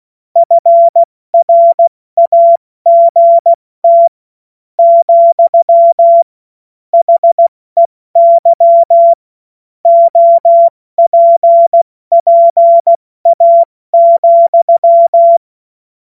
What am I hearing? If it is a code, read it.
Morse code, heard as a keyed tone, about 16 wpm, 680 Hz. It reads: FRAGT , HEY OPPA,
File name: Morsekurs / Witze / Witz 14.November 2023 70 F80